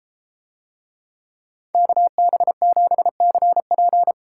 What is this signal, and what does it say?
Morse code, heard as a keyed tone, about 32 wpm, 700 Hz. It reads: KB7CP